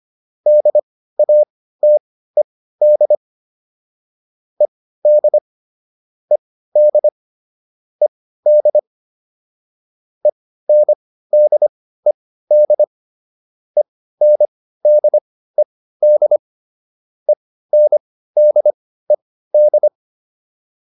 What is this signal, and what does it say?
Morse code, heard as a keyed tone, about 25 wpm, spaced as for 15 wpm, 600 Hz. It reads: DATED ED ED ED ENDED ENDED ENDED